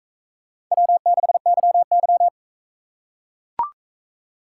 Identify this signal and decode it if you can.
Morse code, heard as a keyed tone, about 40 wpm, 700 Hz. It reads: W6YY